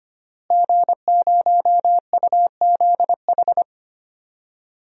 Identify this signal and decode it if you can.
Morse code, heard as a keyed tone, about 25 wpm, 700 Hz. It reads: G0UZH
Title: DXCC-100 Call Signs - 5 Characters - Repeated 3x - 25wpm